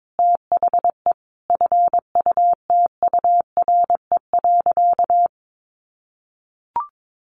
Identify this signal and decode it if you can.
Morse code, heard as a keyed tone, about 22 wpm, 700 Hz. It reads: THE FUTURE.
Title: Sentences from Top 700 Words 22wpm